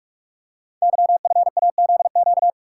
Morse code, heard as a keyed tone, about 44 wpm, 700 Hz. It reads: YFAZQ
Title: Code Groups - 5 Letter:Numbers - 45wpm